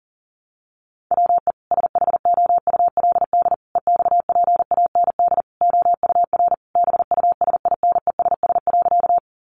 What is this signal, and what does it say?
Morse code, heard as a keyed tone, about 37 wpm, 700 Hz. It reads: WE SHOULD EXPAND OUR BUSINESS.
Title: Sentences from Top 1400 Words 40wpm